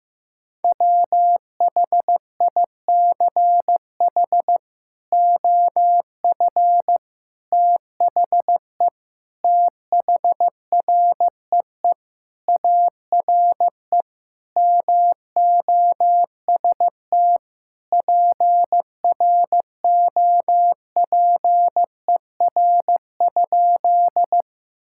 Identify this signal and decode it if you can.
Morse code, heard as a keyed tone, about 16 wpm, 700 Hz. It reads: WHICH OF THE THREE ARE MOST PROPER?
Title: Sentences from Top 1000 Words - Review Part 5 - 15wpm